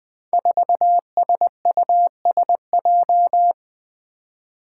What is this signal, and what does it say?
Morse code, heard as a keyed tone, about 20 wpm, 700 Hz. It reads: 4SUSJ
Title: Five Challenging Letter-Number Combinations 20wpm